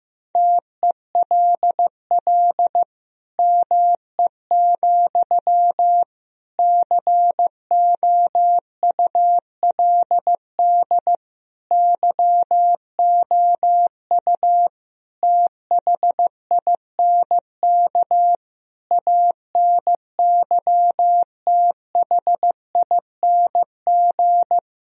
Morse code, heard as a keyed tone, about 15 wpm, 700 Hz. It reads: TELL ME, COULD YOU THINK ANYTHING
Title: Sentences from Top 1000 Words - Review Part 2 - 15wpm